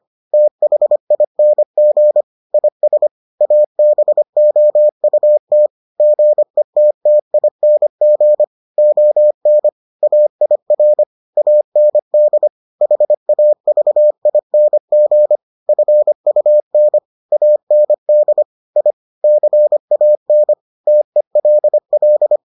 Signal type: Morse code

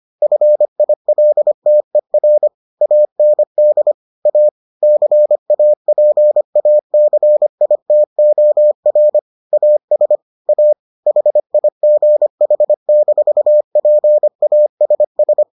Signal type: Morse code

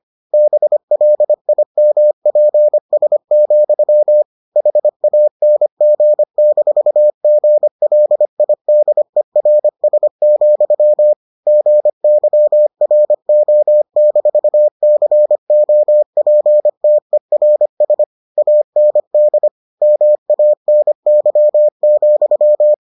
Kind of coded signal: Morse code

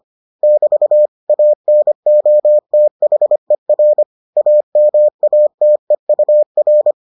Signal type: Morse code

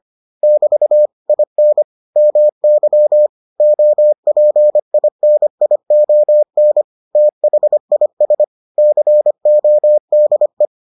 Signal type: Morse code

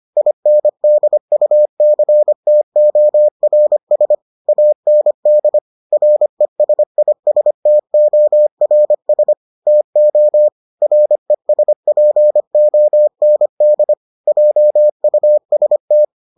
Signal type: Morse code

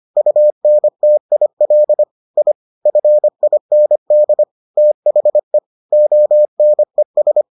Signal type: Morse code

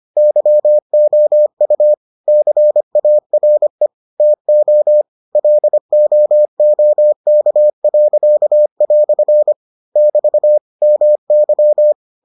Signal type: Morse code

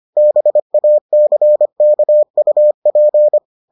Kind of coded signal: Morse code

